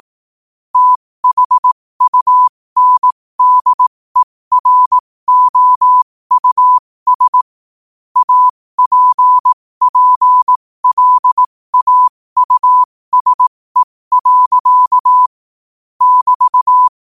Morse code, heard as a keyed tone, about 18 wpm, spaced as for 15 wpm, 1000 Hz. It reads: THUNDEROUS APPLAUSE. <BT>
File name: qotdfiles / 15wpm